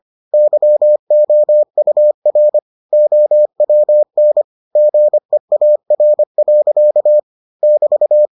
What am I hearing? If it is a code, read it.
Morse code, heard as a keyed tone, about 25 wpm, 600 Hz. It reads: YOUR OWN GEAR. <BT>